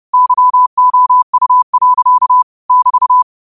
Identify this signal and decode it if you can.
Morse code, heard as a keyed tone, about 30 wpm, 1000 Hz. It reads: YOU. <BT>